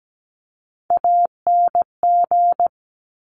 Morse code, heard as a keyed tone, about 17 wpm, 700 Hz. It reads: ANG
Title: Top Three Letter Combinations in Words 17wpm